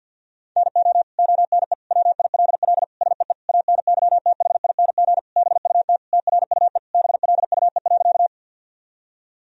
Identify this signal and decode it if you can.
Morse code, heard as a keyed tone, about 50 wpm, 700 Hz. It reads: NO ONE WILL SEE ANYTHING BUT TRUE BLUE.
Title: Sentences from Top 1000 Words - Review Part 3 - 50wpm